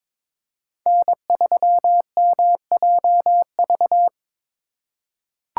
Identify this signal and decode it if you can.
Morse code, heard as a keyed tone, about 22 wpm, 700 Hz. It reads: N3MJV
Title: USA Call Signs - 5 Characters - 2x Character Spacing - 22wpm